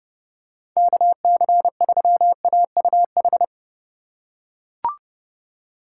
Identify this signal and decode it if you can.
Morse code, heard as a keyed tone, about 29 wpm, 700 Hz. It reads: KC3AUH